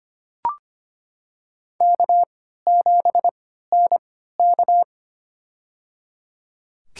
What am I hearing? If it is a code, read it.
Morse code, heard as a keyed tone, about 25 wpm, spaced as for 14 wpm, 700 Hz. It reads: K7NK
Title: USA Call Signs - 4 Characters - 3x Character Spacing - 25wpm